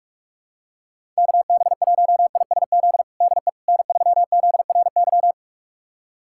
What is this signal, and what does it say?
Morse code, heard as a keyed tone, about 43 wpm, 700 Hz. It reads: KB1ISZ DE N3ZRY
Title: USA Callsign DE Callsign 45wpm